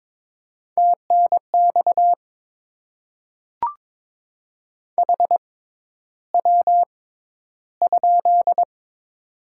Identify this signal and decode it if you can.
Morse code, heard as a keyed tone, about 22 wpm, spaced as for 11 wpm, 690 Hz. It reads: TNX HW?
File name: Top 20 QSO Elements - 6x Character Spacing - 22wpm